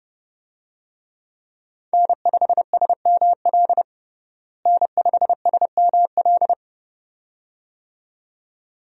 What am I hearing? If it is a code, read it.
Morse code, heard as a keyed tone, about 30 wpm, 700 Hz. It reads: N5SML N5SML
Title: USA Call Signs - 5 Characters - Repeated 2x - 30wpm